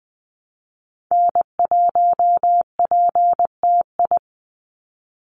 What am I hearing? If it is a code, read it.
Morse code, heard as a keyed tone, about 20 wpm, 700 Hz. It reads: N1PTI
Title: USA Call Signs 20wpm